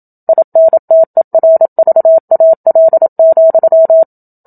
Morse code, heard as a keyed tone, about 27 wpm, 650 Hz. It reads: INTERVAL,
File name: Book Two / 01-Chapter 1 - 27 WPM